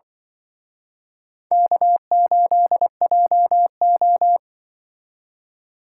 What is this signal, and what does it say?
Morse code, heard as a keyed tone, about 24 wpm, 700 Hz. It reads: K8JO
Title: Speed Racing - USA 1x2 Call Signs 20wpm